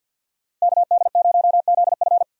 Morse code, heard as a keyed tone, about 50 wpm, 700 Hz. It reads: KD0ZF
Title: USA Call Signs - 5 Characters - Repeated 3x - 50wpm